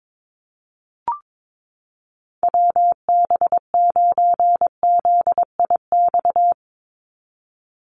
Morse code, heard as a keyed tone, about 22 wpm, 700 Hz. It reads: WB9ZIX
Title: USA Call Signs 22wpm